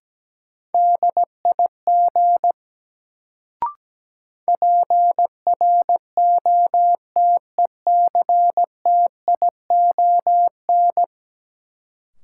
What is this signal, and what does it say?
Morse code, heard as a keyed tone, about 17 wpm, 700 Hz. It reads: DIG PROTECTION
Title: Sentences from Top 1600 Words - New Word Drill - 17wpm